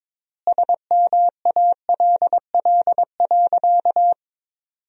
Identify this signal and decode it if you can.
Morse code, heard as a keyed tone, about 22 wpm, 700 Hz. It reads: SMALL.
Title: 3x Word-Spacing - Sentences from Top 400 Words - 22wpm